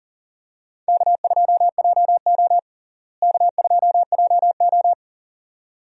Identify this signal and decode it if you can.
Morse code, heard as a keyed tone, about 40 wpm, 700 Hz. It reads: K2JO K2JO
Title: USA Call Signs - 4 Characters - Repeated 2x - 40wpm